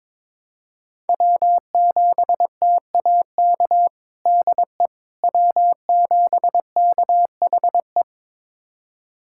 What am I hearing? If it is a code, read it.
Morse code, heard as a keyed tone, about 22 wpm, 700 Hz. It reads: W7TAK DE W7KHE
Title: USA Callsign DE Callsign - 5 Characters - 22wpm